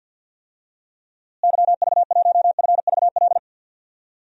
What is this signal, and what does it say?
Morse code, heard as a keyed tone, about 53 wpm, 700 Hz. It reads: YV1FVL
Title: DXCC-100 Call Signs - 6 Characters - Repeated 2x - 50wpm